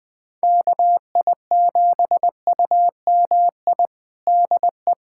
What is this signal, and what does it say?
Morse code, heard as a keyed tone, about 20 wpm, 700 Hz. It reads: KI7UMI DE
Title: USA Callsign DE Callsign - 6 Characters - 20wpm